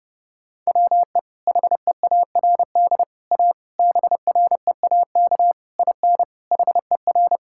Morse code, heard as a keyed tone, about 29 wpm, 700 Hz. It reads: WE HEARD A BREAK IN HER
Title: Sentences from Top 1000 Words - Review Part 6 - 30wpm